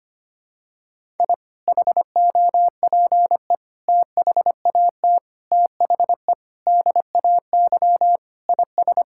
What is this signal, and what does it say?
Morse code, heard as a keyed tone, about 25 wpm, 700 Hz. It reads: I HOPE THAT THE DAY IS
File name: practice / 25wpm / Sentences with 2-Word Phrases - 25wpm